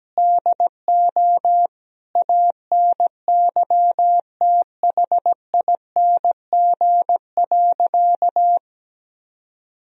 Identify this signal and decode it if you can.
Morse code, heard as a keyed tone, about 17 wpm, 700 Hz. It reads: DO ANYTHING.
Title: Sentences from Top 1000 Words - Review Part 2 - 17wpm